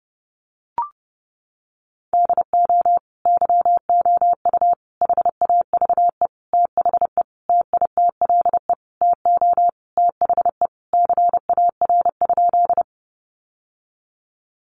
Morse code, heard as a keyed tone, about 30 wpm, 700 Hz. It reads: DO YOU HAVE THE TITLE TO THE CAR?
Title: Sentences from Top 1200 Words 30wpm